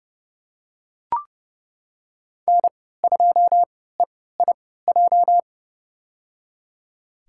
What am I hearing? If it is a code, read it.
Morse code, heard as a keyed tone, about 32 wpm, spaced as for 17 wpm, 700 Hz. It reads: N2EIJ
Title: USA Call Signs - 5 Characters - 3x Character Spacing - 30wpm